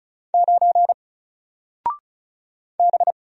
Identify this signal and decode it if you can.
Morse code, heard as a keyed tone, about 36 wpm, 700 Hz. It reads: 9 B
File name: Single-Letter-Number-35wpm